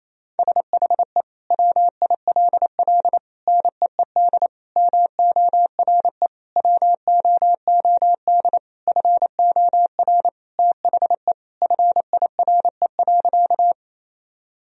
Morse code, heard as a keyed tone, about 28 wpm, 700 Hz. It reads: SHE WILL NEED MORE WOOD FOR THE FIRE.
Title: Sentences from Top 1000 Words - Review Part 2 - 28wpm